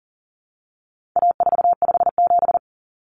Morse code, heard as a keyed tone, about 40 wpm, 700 Hz. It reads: A457
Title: Common Cut Numbers - 4-Digit 40wpm